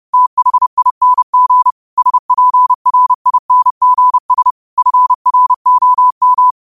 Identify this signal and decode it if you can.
Morse code, heard as a keyed tone, about 30 wpm, 1000 Hz. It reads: THING SPRINGS FROM